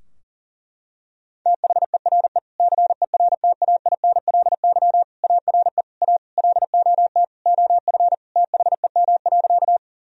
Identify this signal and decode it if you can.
Morse code, heard as a keyed tone, about 40 wpm, 700 Hz. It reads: THERE CERTAINLY ARE A LOT OF THEM.
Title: Sentences from Top 1000 Words - Review Part 3 - 40wpm